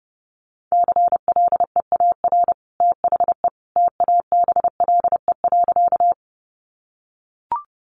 Morse code, heard as a keyed tone, about 28 wpm, 700 Hz. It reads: CLEAR THE TABLE.
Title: Sentences from Top 500 Words 30wpm